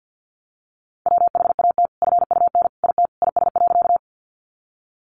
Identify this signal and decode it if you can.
Morse code, heard as a keyed tone, about 50 wpm, 700 Hz. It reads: WHAT FUN IT IS.
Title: Sentences from Top 700 Words 50wpm